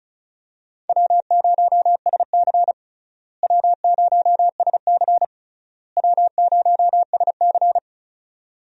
Morse code, heard as a keyed tone, about 34 wpm, 700 Hz. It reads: W0SC W0SC W0SC
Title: USA Call Signs - 4 Characters - Repeated 3x - 35wpm